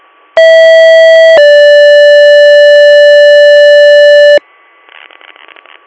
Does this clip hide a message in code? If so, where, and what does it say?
Yes: two-tone page from 0.4 to 4.4 s, heard as a one-second tone, then a three-second tone, about 650 then 580 Hz, with noise.